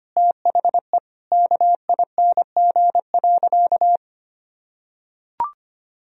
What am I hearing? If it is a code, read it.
Morse code, heard as a keyed tone, about 25 wpm, 700 Hz. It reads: THE KING.
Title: Sentences from Top 400 Words and Abbreviations 25wpm